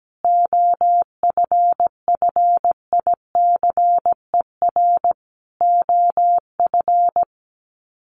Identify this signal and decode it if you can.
Morse code, heard as a keyed tone, about 17 wpm, 700 Hz. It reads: OFFICER OF
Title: Sets of 5 Words Encore2 17wpm